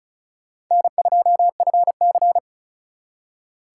Morse code, heard as a keyed tone, about 45 wpm, 700 Hz. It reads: N2FC